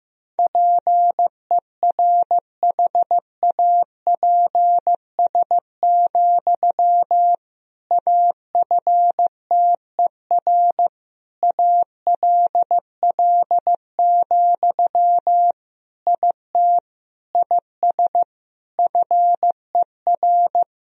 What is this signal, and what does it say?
Morse code, heard as a keyed tone, about 15 wpm, 700 Hz. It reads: PERHAPS, AFTER ALL, IT IS FER
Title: Sentences from Top 500 Words and Abbreviations 15wpm